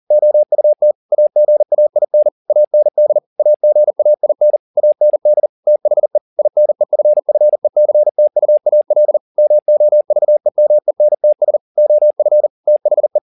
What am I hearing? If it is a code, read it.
Morse code, heard as a keyed tone, about 40 wpm, 600 Hz. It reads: OUT AGAIN AND AGAIN AND THE INEFFECTUAL MOVEMENTS OF THE